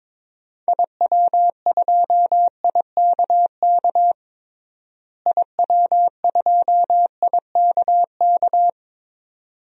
Morse code, heard as a keyed tone, about 22 wpm, 700 Hz. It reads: IW2IKK IW2IKK
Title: DXCC-100 Call Signs - 6 Characters - Repeated 2x - 22wpm